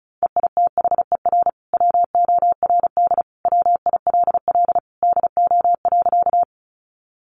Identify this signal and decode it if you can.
Morse code, heard as a keyed tone, about 34 wpm, 700 Hz. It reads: EITHER WORD WILL DO.
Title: Sentences from Top 600 Words 35wpm